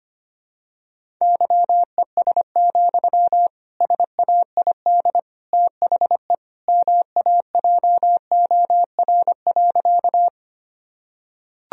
Morse code, heard as a keyed tone, about 25 wpm, 700 Hz. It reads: YES, SAID THE MAJOR.
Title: Sentences from Top 1000 Words - Review Part 4 - 25wpm